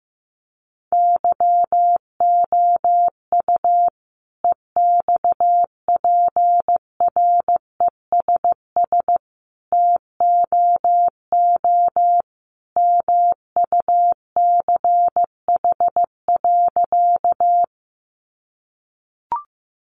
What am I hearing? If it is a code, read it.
Morse code, heard as a keyed tone, about 15 wpm, 700 Hz. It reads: YOU EXPRESS TOO MUCH.
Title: Sentences from Top 1100 Words 15wpm